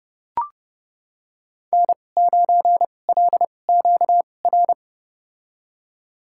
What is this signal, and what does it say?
Morse code, heard as a keyed tone, about 30 wpm, spaced as for 22 wpm, 700 Hz. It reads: N9LQR
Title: USA Call Signs - 5 Characters - 2x Character Spacing - 30wpm